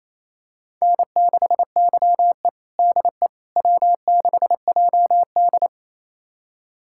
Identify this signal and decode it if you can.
Morse code, heard as a keyed tone, about 27 wpm, 700 Hz. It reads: N6YE DE W6JD
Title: USA Callsign DE Callsign - 4 Characters - 28wpm